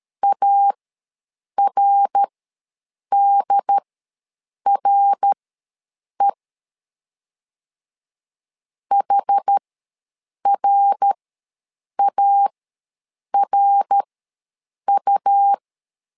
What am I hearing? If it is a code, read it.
Morse code, heard as a keyed tone, about 13 wpm, spaced as for 7 wpm, 800 Hz. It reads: ARDRE HRARU
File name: LEZIONI / 3 MWRO / 3 MWR ws 12wpm 1809